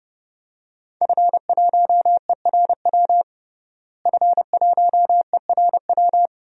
Morse code, heard as a keyed tone, about 32 wpm, 700 Hz. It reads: F1ERW F1ERW
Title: DXCC-100 Call Signs - 5 Characters - Repeated 2x - 30wpm